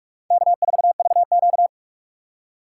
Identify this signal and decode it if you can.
Morse code, heard as a keyed tone, about 57 wpm, 700 Hz. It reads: K4VQ